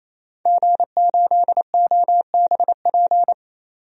Morse code, heard as a keyed tone, about 27 wpm, 700 Hz. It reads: G8OBP
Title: DXCC-100 Call Signs - 5 Characters - 28wpm